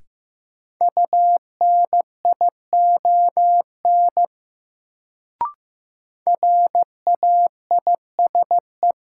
Morse code, heard as a keyed tone, about 15 wpm, 700 Hz. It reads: UNION RAISE
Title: Sentences from Top 1400 Words - New Word Drill - 15wpm